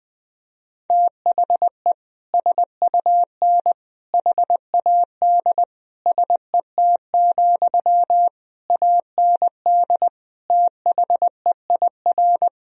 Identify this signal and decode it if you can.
Morse code, heard as a keyed tone, about 21 wpm, 700 Hz. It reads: THE SUN HAD SET, AND THEIR
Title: Sentences from Top 1000 Words - Review Part 1 - 20wpm